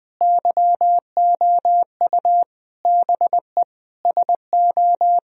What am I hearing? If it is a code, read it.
Morse code, heard as a keyed tone, about 21 wpm, 700 Hz. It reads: YOU BE SO
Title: Sentences from Top 1000 Words - Review Part 2 - 20wpm